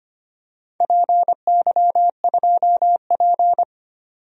Morse code, heard as a keyed tone, about 25 wpm, 700 Hz. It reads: PY2P
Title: DXCC-100 Call Signs - 4 Characters - 25wpm